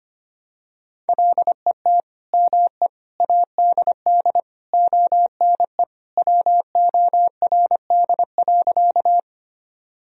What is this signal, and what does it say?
Morse code, heard as a keyed tone, about 25 wpm, 700 Hz. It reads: LET ME ADD ONE WORD.
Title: Sentences from Top 1000 Words - Review Part 1 - 25wpm